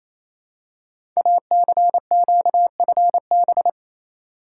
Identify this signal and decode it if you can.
Morse code, heard as a keyed tone, about 28 wpm, 700 Hz. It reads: ACQFB
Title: Code Groups - 5 Letters - 28wpm